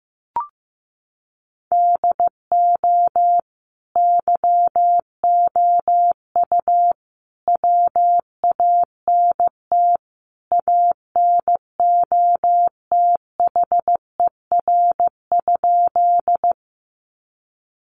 Morse code, heard as a keyed tone, about 15 wpm, 700 Hz. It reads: DO YOU WANT ANOTHER?
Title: Sentences from Top 600 Words 15wpm